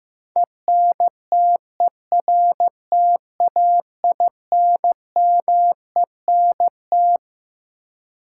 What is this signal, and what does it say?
Morse code, heard as a keyed tone, about 15 wpm, 700 Hz. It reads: ENTERTAINMENT
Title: Sentences from Top 1000 Words - Review Part 6 - New Word Drill - 15wpm